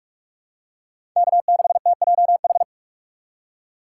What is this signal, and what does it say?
Morse code, heard as a keyed tone, about 42 wpm, 700 Hz. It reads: K6TJH
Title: USA Call Signs - 6x Character Spacing - 45wpm